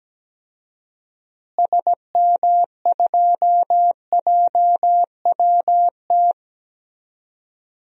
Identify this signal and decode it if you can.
Morse code, heard as a keyed tone, about 17 wpm, 700 Hz. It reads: SM2JWT
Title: DXCC-100 Call Signs - 6 Characters - Repeated 2x - 17wpm